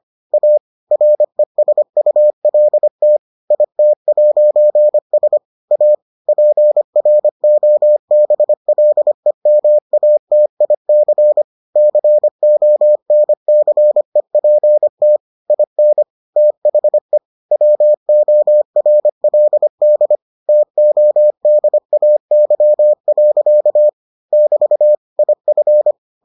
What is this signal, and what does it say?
Morse code, heard as a keyed tone, about 25 wpm, 600 Hz. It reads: A RESULT IT'S A PROBLEMATIC CONCEPT IN THE WORLD TODAY. <BT> IF